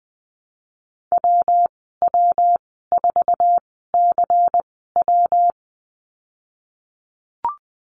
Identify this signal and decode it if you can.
Morse code, heard as a keyed tone, about 19 wpm, spaced as for 14 wpm, 700 Hz. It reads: WW4CW
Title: 122 Lesson 20g - Number 3 - Call Signs 20wpm